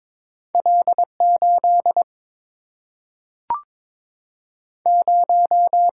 Morse code, heard as a keyed tone, about 22 wpm, 700 Hz. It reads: L 8 0
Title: Single-Letter Single-Number 22wpm